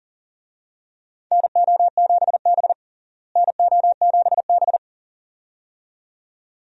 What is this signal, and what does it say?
Morse code, heard as a keyed tone, about 40 wpm, 700 Hz. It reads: NO7B NO7B
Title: USA Call Signs - 4 Characters - Repeated 2x - 40wpm